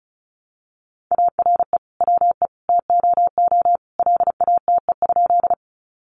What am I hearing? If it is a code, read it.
Morse code, heard as a keyed tone, about 34 wpm, 700 Hz. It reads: ARE WE TOO LATE?
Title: Sentences from Top 300 Words 35wpm